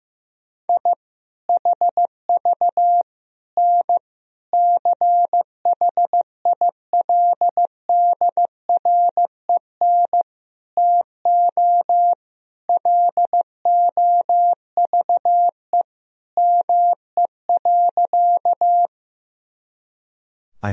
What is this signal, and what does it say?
Morse code, heard as a keyed tone, about 15 wpm, 700 Hz. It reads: I HV N CHILDREN TO LOVE ME.
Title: Sentences from Top 400 Words and Abbreviations 15wpm